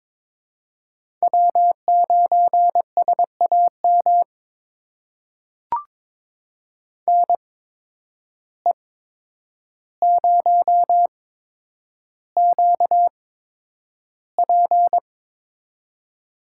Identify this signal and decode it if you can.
Morse code, heard as a keyed tone, about 22 wpm, spaced as for 9 wpm, 700 Hz. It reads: W9SAM NE0QP
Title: USA Call Signs - 5 Characters - 8x Character Spacing - 22wpm